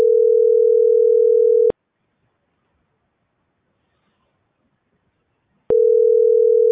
Ringback tone (North America), 440 + 480 Hz.